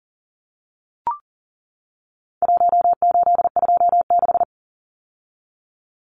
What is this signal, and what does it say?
Morse code, heard as a keyed tone, about 40 wpm, 700 Hz. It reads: 1826
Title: Numbers - Contest Style - 4-Digit 40wpm